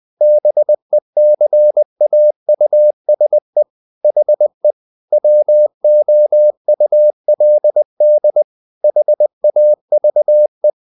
Morse code, heard as a keyed tone, about 20 wpm, 600 Hz. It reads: BECAUSE HE WOULD HAVE
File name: the old man and the sea / the old man and the sea part 1 20wpm